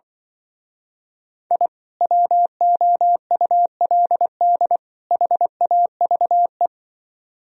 Morse code, heard as a keyed tone, about 25 wpm, 700 Hz. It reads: I WOULD HAVE
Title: Speed-Racing - Sets of 3 Words 20wpm